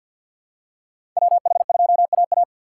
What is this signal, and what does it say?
Morse code, heard as a keyed tone, about 52 wpm, 700 Hz. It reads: WH2AA
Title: USA Call Signs - 6x Character Spacing - 50wpm